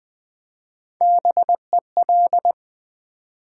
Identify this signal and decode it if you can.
Morse code, heard as a keyed tone, about 20 wpm, 700 Hz. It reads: BEL
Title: Build A Word - 3 Letters At A Time - 6-Letter Words - 20wpm